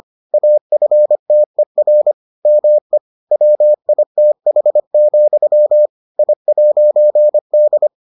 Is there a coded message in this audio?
Yes: Morse code from 0.3 to 7.9 s, heard as a keyed tone, about 25 wpm, 600 Hz. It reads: AFTER ME WITH, I'D